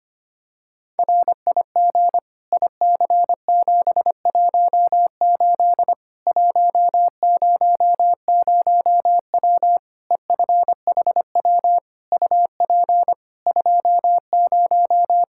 Morse code, heard as a keyed tone, about 25 wpm, 700 Hz. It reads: RIG IC718 100W EFHW UP 20
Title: Practice QSO - Exchange 2 - Info - 25wpm